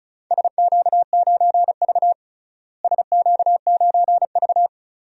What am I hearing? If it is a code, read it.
Morse code, heard as a keyed tone, about 36 wpm, 700 Hz. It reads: SQ9V SQ9V
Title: DXCC-100 Call Signs - 4 Characters - Repeated 2x - 35wpm